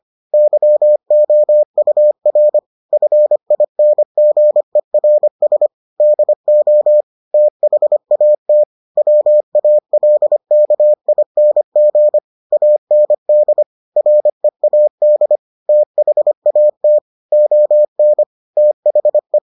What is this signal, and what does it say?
Morse code, heard as a keyed tone, about 25 wpm, 600 Hz. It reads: YOUR FINGERS DO THAT WALKING AND READ THAT ON THE